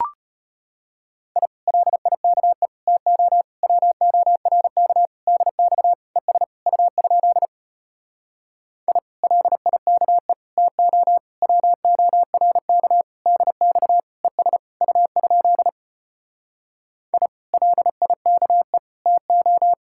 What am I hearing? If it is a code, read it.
Morse code, heard as a keyed tone, about 33 wpm, 700 Hz. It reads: I LIKE TO WORK DX ES U? I LIKE TO WORK DX ES U? I LIKE TO